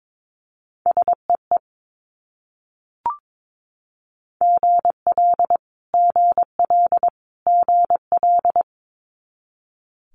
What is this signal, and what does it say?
Morse code, heard as a keyed tone, about 22 wpm, 700 Hz. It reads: SEE GL GL GL
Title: Multi-letter Phonemes Repeated 3x 22wpm